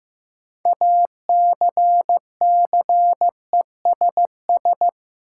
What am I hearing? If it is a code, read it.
Morse code, heard as a keyed tone, about 15 wpm, 700 Hz. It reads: ACCESS
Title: Sentences from Top 1000 Words - Review Part 4 - New Word Drill - 15wpm